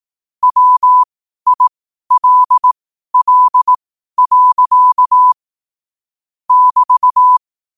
Morse code, heard as a keyed tone, about 18 wpm, spaced as for 12 wpm, 1000 Hz. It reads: WILL. <BT>
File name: qotdfiles / 12wpm